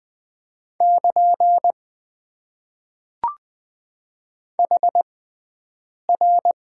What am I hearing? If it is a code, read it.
Morse code, heard as a keyed tone, about 20 wpm, spaced as for 7 wpm, 700 Hz. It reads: <KN> HR
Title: Top 90 QSO Elements - 6x Character Spacing - 20wpm